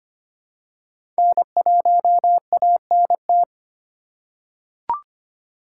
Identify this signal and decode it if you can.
Morse code, heard as a keyed tone, about 25 wpm, 700 Hz. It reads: N1ANT
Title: USA Call Signs - 5 Characters - 3x Character Spacing - 25wpm